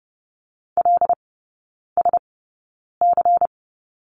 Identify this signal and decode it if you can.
Morse code, heard as a keyed tone, about 30 wpm, spaced as for 9 wpm, 700 Hz. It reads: LSC
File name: Lesson 18e - Letter m - Three Characters 30wpm